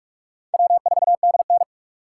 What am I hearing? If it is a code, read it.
Morse code, heard as a keyed tone, about 41 wpm, 700 Hz. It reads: W4DN